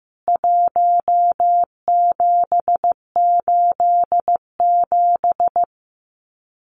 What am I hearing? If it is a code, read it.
Morse code, heard as a keyed tone, about 15 wpm, 700 Hz. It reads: 1787
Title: CWT Exchange 15wpm